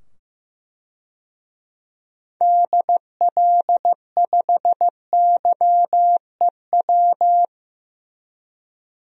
Morse code, heard as a keyed tone, about 15 wpm, 700 Hz. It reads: DL5YEW
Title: DXCC-100 Call Signs - 6 Characters - 15wpm